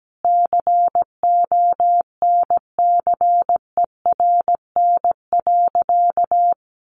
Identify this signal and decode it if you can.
Morse code, heard as a keyed tone, about 17 wpm, 700 Hz. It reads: CONCERN.
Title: Sentences from Top 1400 Words 17wpm